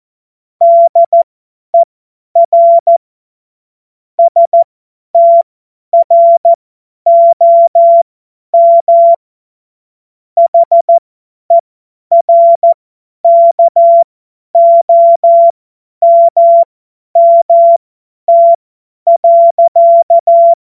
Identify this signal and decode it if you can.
Morse code, heard as a keyed tone, about 14 wpm, spaced as for 10 wpm, 680 Hz. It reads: DER STROM HERKOMMT.
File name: Morsekurs / Witze / Witz 07. Februar 2023 50 F70